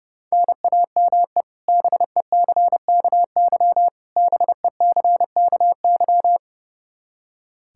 Morse code, heard as a keyed tone, about 32 wpm, 700 Hz. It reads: NAME BECKY BECKY
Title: K1USN SST QSO Practice 30wpm